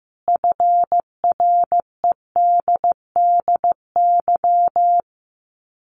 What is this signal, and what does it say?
Morse code, heard as a keyed tone, about 15 wpm, 700 Hz. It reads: FREDDY
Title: Common QSO Phrases 15wpm